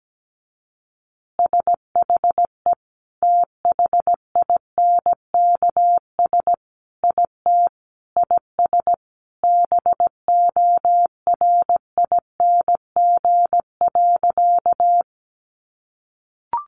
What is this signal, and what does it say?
Morse code, heard as a keyed tone, about 17 wpm, 700 Hz. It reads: SHE THINKS IT IS BORING.
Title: Easy Phrases 17wpm